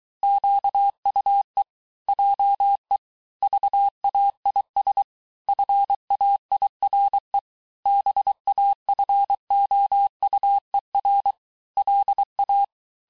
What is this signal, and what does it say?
Morse code, heard as a keyed tone, about 23 wpm, 800 Hz. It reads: QUE JE VAIS FAIRE BAFOUER LA
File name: cours-du-mercredi / audios / COURS-855--texte-clair-vitesse-22-mpm